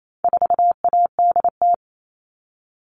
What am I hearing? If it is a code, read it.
Morse code, heard as a keyed tone, about 28 wpm, 700 Hz. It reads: 4ADT
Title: All Cut Numbers - 4-Digit 28wpm